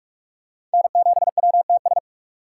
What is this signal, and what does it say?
Morse code, heard as a keyed tone, about 45 wpm, 700 Hz. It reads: N7WTS
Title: USA Call Signs - 2x Character Spacing - 45wpm